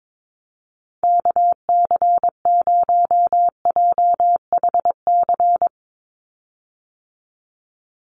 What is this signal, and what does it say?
Morse code, heard as a keyed tone, about 22 wpm, 700 Hz. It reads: KC0JHC
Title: USA Call Signs 22wpm